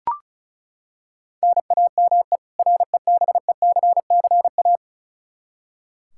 Morse code, heard as a keyed tone, about 36 wpm, 700 Hz. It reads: NAME REBECCA
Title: K1USN SST QSO Practice 35wpm